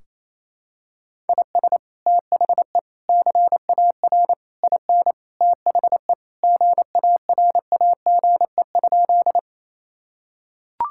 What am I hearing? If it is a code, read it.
Morse code, heard as a keyed tone, about 27 wpm, 700 Hz. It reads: IS THE CAR IN THE GARAGE?
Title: Sentences from Top 1600 Words 28wpm